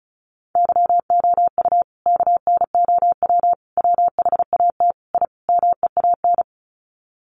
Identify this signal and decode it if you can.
Morse code, heard as a keyed tone, about 34 wpm, 700 Hz. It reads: YOU KNOW WHAT I MEAN
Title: Sets of 5 Words 35wpm